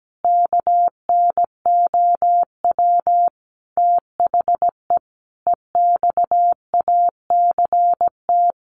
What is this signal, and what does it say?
Morse code, heard as a keyed tone, about 17 wpm, 700 Hz. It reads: KNOW THE EXACT
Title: Sentences from Top 1300 Words 17wpm